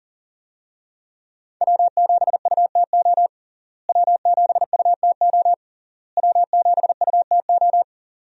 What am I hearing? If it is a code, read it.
Morse code, heard as a keyed tone, about 37 wpm, 700 Hz. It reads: W7UTO W7UTO W7UTO
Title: USA Call Signs - 5 Characters - Repeated 3x - 40wpm